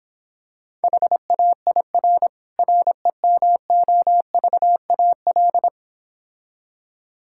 Morse code, heard as a keyed tone, about 26 wpm, 700 Hz. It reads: HAIR REMOVAL